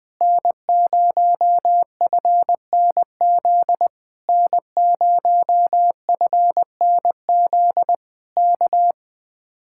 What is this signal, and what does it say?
Morse code, heard as a keyed tone, about 21 wpm, 700 Hz. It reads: N0FNZ N0FNZ K
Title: Speed-Racing Common QSO Phrases - 20wpm